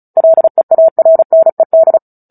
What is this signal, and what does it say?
Morse code, heard as a keyed tone, about 34 wpm, 650 Hz. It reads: LEARNED
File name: Book Two / 09-Chapter 9 - 35 WPM